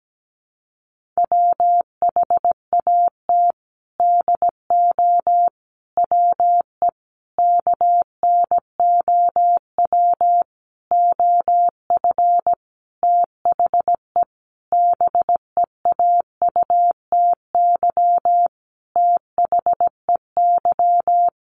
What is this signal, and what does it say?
Morse code, heard as a keyed tone, about 17 wpm, 700 Hz. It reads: WHAT DO WE KNOW OF THE BEAUTY THEY
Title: Sentences from Top 1400 Words 17wpm